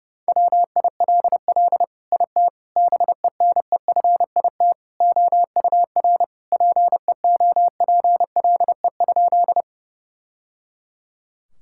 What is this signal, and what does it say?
Morse code, heard as a keyed tone, about 30 wpm, 700 Hz. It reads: WILL IT BENEFIT OUR PEOPLE?
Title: Sentences from Top 1000 Words - Review Part 6 - 30wpm